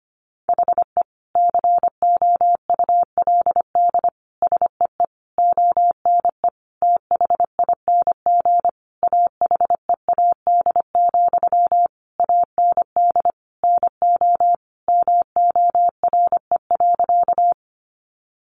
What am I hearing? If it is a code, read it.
Morse code, heard as a keyed tone, about 25 wpm, 700 Hz. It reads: HE COULD SEE ONE THING AHEAD, AND NO MORE.
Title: Sentences from Top 1100 Words 25wpm